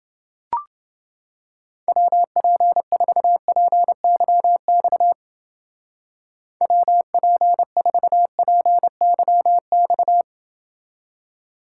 Morse code, heard as a keyed tone, about 29 wpm, 700 Hz. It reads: WP4PYX WP4PYX